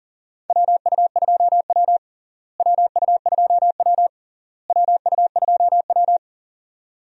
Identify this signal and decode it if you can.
Morse code, heard as a keyed tone, about 40 wpm, 700 Hz. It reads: WU2W WU2W WU2W
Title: USA Call Signs - 4 Characters - Repeated 3x - 40wpm